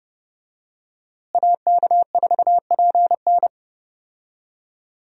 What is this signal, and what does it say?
Morse code, heard as a keyed tone, about 30 wpm, 700 Hz. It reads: AK4PN
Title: USA Call Signs - 5 Characters - Repeated 2x - 30wpm